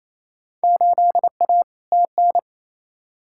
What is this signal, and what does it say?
Morse code, heard as a keyed tone, about 28 wpm, 700 Hz. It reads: 8A TN
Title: Field Day Exchanges Encore - 28wpm